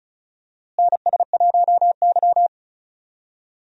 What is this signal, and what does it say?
Morse code, heard as a keyed tone, about 36 wpm, 700 Hz. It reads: NS1Y
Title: USA Call Signs - 4 Characters - Repeated 3x - 35wpm